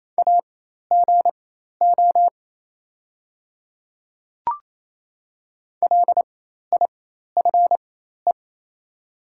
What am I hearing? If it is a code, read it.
Morse code, heard as a keyed tone, about 28 wpm, spaced as for 13 wpm, 700 Hz. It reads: AGO LIFE
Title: 4x Character Spacing - Top 500 Words - 28wpm